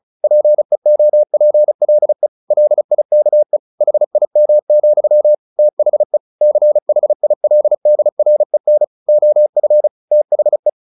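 Morse code, heard as a keyed tone, about 34 wpm, 600 Hz. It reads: PEOPLE LIKE HIM, THE CHILDREN OF THE
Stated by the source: Morse practice at 35 wpm from CW med Gnister, sending Ken Follett's novel Never in English.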